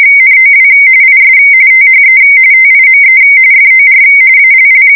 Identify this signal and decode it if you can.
RTTY, about 45.45 baud. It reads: NOW IS THE TIME FOR ALL
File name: ws1sm / Sound Sample RTTY-45